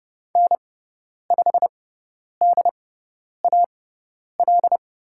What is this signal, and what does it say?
Morse code, heard as a keyed tone, about 30 wpm, spaced as for 10 wpm, 700 Hz. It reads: N5DAL
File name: Lesson 16g - Letter c - Call Signs 30wpm